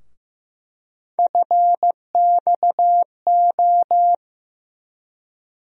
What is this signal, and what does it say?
Morse code, heard as a keyed tone, about 15 wpm, 700 Hz. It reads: FXO
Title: USA Call Sign Postfixes - 3 Letters 15wpm